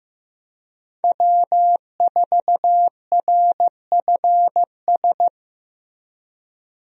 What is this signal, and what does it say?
Morse code, heard as a keyed tone, about 15 wpm, 700 Hz. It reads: W4RFS
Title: USA Call Signs - 5 Characters - Repeated 2x - 15wpm